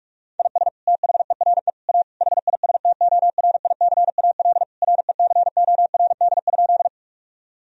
Morse code, heard as a keyed tone, about 41 wpm, 700 Hz. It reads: IS THERE A HISTORICAL RECORD?
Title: Sentences from Top 1000 Words - Review Part 4 - 45wpm